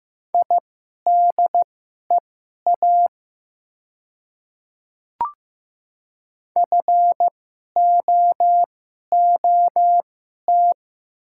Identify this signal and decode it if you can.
Morse code, heard as a keyed tone, about 15 wpm, spaced as for 11 wpm, 700 Hz. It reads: IDEA FOOT
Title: 2x Character Spacing - Top 500 Words - 15wpm